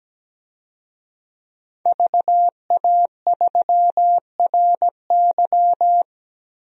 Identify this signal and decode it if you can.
Morse code, heard as a keyed tone, about 17 wpm, 700 Hz. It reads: VA3RY
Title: DXCC-100 Call Signs - 5 Characters - 17wpm